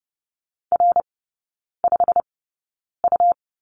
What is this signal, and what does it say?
Morse code, heard as a keyed tone, about 30 wpm, spaced as for 9 wpm, 690 Hz. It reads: R5U
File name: Lesson 17e - Letter u - Three Characters 30wpm